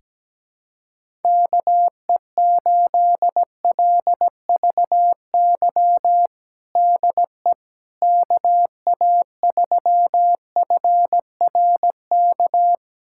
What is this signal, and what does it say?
Morse code, heard as a keyed tone, about 17 wpm, 700 Hz. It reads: KE8LVY DE KA3FRK
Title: USA Callsign DE Callsign - 6 Characters - 17wpm